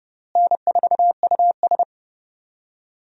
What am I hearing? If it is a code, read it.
Morse code, heard as a keyed tone, about 30 wpm, 700 Hz. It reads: N4US
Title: USA Call Signs - 4 Characters - Repeated 3x - 30wpm